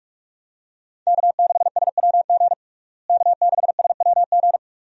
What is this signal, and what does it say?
Morse code, heard as a keyed tone, about 46 wpm, 700 Hz. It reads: K6SWG K6SWG
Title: USA Call Signs - 5 Characters - Repeated 3x - 45wpm